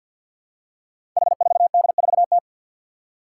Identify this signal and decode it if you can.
Morse code, heard as a keyed tone, about 53 wpm, 700 Hz. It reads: H4D4T
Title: Five Challenging Letter-Number Combinations 50wpm